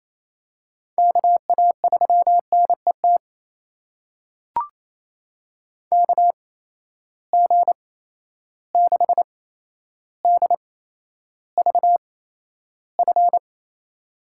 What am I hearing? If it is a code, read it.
Morse code, heard as a keyed tone, about 28 wpm, spaced as for 12 wpm, 700 Hz. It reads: KA3NET KG6DVF
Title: USA Call Signs - 6 Characters - 8x Character Spacing - 28wpm